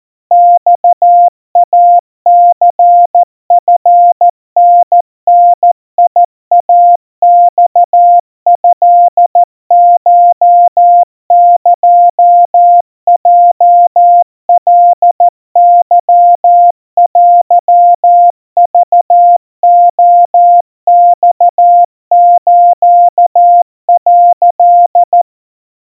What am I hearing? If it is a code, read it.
Morse code (Japanese Wabun code), heard as a keyed tone, about 14 wpm, 700 Hz. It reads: マイニチタダイマトコエヲカケテクレマス」
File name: kawa / Morse sample / kamyu 50 cpm 700 Hz Crispy